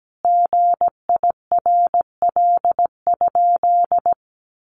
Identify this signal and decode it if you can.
Morse code, heard as a keyed tone, about 17 wpm, 700 Hz. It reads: GIRL?
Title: Sentences from Top 1500 Words 17wpm